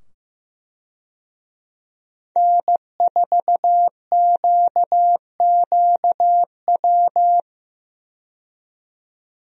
Morse code, heard as a keyed tone, about 15 wpm, 700 Hz. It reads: N4QQW
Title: USA Call Signs - 5 Characters - 15wpm